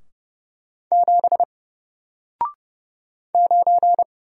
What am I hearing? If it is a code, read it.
Morse code, heard as a keyed tone, about 30 wpm, 700 Hz. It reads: 7 9
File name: Single-Letter-Number-30wpm